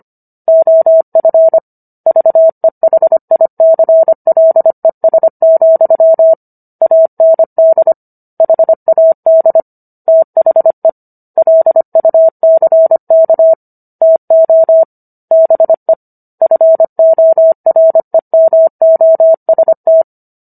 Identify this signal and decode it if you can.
Morse code, heard as a keyed tone, about 25 wpm, 650 Hz. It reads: OF VEHICLES, AND HAD THE LUCK TO BE FOREMOST